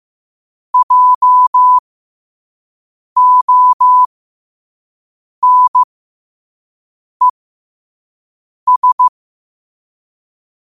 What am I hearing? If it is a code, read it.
Morse code, heard as a keyed tone, about 15 wpm, spaced as for 5 wpm, 1000 Hz. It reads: JONES
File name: qotdfiles / 5wpm